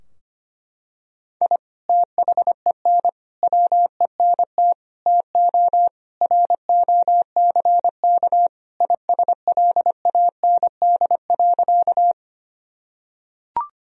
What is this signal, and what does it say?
Morse code, heard as a keyed tone, about 25 wpm, 700 Hz. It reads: I THEN WENT TO ROCK ISLAND.
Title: Sentences from Top 1000 Words - Review Part 3 - 25wpm